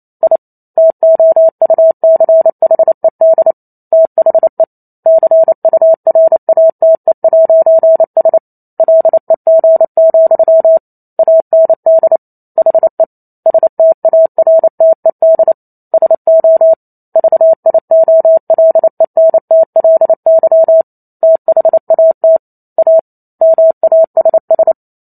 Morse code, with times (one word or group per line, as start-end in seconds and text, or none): 0.2-0.4 I
0.8-3.5 TOUCHED
3.9-4.6 THE
5.1-8.4 CURATE'S
8.8-10.8 LEG,
11.2-12.2 AND
12.6-13.0 HE
13.5-15.5 STARTED
15.9-16.7 SO
17.2-20.8 VIOLENTLY
21.2-22.4 THAT
22.8-23.0 A
23.4-24.7 MASS